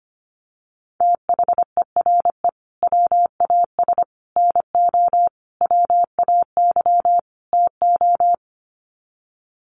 Morse code, heard as a keyed tone, about 25 wpm, 700 Hz. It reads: THERE WAS NO WAY TO
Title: Sets of 5 Words Encore 25wpm